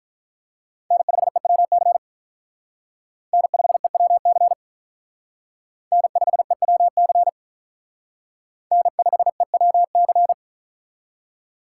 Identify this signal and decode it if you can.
Morse code, heard as a keyed tone, about 43 wpm, 700 Hz. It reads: N5EWC N5EWC N5EWC N5EWC
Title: Speed Racing - USA 1x3 Call Signs 35wpm